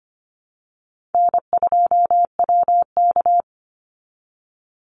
Morse code, heard as a keyed tone, about 25 wpm, 700 Hz. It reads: N2WK
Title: Easy Call Signs Repeated 2x - 25wpm